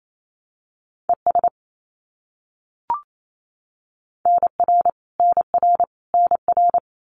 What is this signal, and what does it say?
Morse code, heard as a keyed tone, about 28 wpm, 700 Hz. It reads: ES NR NR NR
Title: Abbreviations Repeated 3x - 28wpm